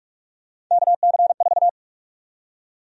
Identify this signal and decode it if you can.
Morse code, heard as a keyed tone, about 47 wpm, 700 Hz. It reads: KC4
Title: USA Call Sign Prefixes - 45wpm